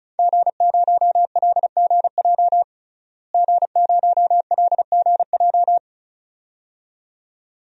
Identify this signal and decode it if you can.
Morse code, heard as a keyed tone, about 34 wpm, 700 Hz. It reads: G0LGJ G0LGJ
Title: DXCC-100 Call Signs - 5 Characters - Repeated 2x - 35wpm